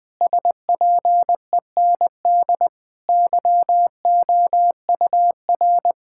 Morse code, heard as a keyed tone, about 20 wpm, 700 Hz. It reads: SPEND YOUR
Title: Sentences from Top 1000 Words - Review Part 5 - 20wpm